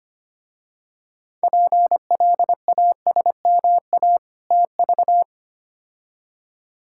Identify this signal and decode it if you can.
Morse code, heard as a keyed tone, about 25 wpm, 700 Hz. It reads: PLASMA TV
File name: practice / 25wpm / Difficult Words - Speed-Racing with Hints - 25wpm